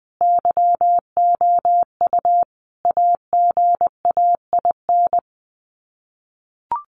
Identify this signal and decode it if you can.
Morse code, heard as a keyed tone, about 20 wpm, 700 Hz. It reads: YOU AGAIN
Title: Sets of 5 Words Encore2 20wpm